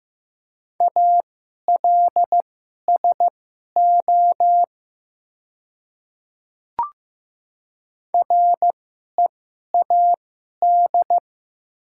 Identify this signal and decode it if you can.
Morse code, heard as a keyed tone, about 15 wpm, spaced as for 11 wpm, 700 Hz. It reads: ALSO READ
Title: 2x Character Spacing - Top 200 Words - 15wpm